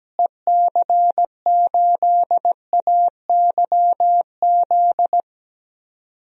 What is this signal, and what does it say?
Morse code, heard as a keyed tone, about 17 wpm, 700 Hz. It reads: EC8AYZ
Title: DXCC-100 Call Signs - 6 Characters - Repeated 3x - 17wpm